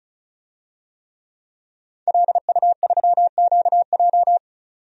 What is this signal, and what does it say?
Morse code, heard as a keyed tone, about 34 wpm, 700 Hz. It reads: LU3QJ
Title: DXCC-100 Call Signs - 5 Characters - 35wpm